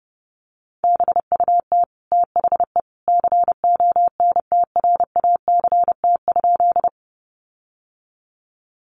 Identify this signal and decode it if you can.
Morse code, heard as a keyed tone, about 30 wpm, 700 Hz. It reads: BUT THE CONTRACT?
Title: Sentences from Top 1400 Words 30wpm